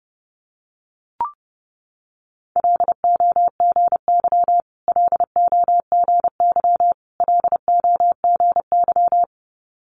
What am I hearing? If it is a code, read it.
Morse code, heard as a keyed tone, about 30 wpm, 700 Hz. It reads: LOGY LOGY LOGY
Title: Multi-letter Phonemes Repeated 3x 30wpm